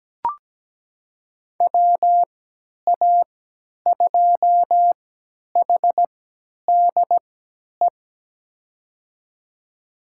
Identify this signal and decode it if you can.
Morse code, heard as a keyed tone, about 17 wpm, spaced as for 10 wpm, 700 Hz. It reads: WA2HDE
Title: USA Call Signs - 6 Characters - 3x Character Spacing - 17wpm